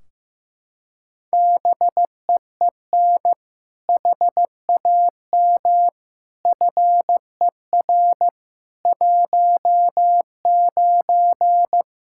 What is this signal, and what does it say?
Morse code, heard as a keyed tone, about 15 wpm, 700 Hz. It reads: BEEN HAM FER 19
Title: Practice QSO - Exchange 3 - More Info - 15wpm